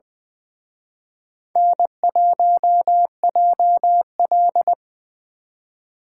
Morse code, heard as a keyed tone, about 24 wpm, 700 Hz. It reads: N1JL
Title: Speed Racing - USA 1x2 Call Signs 20wpm